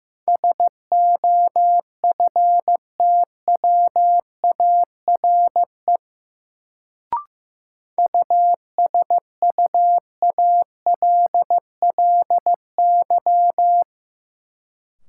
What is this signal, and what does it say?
Morse code, heard as a keyed tone, about 15 wpm, 700 Hz. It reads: SOFTWARE USUALLY
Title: Sentences from Top 1000 Words - Review Part 3 - New Word Drill - 15wpm